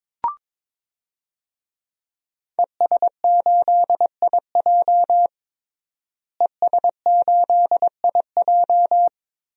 Morse code, heard as a keyed tone, about 22 wpm, 700 Hz. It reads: ES8IJ ES8IJ
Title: DXCC-100 Call Signs - 5 Characters - Repeated 2x - 22wpm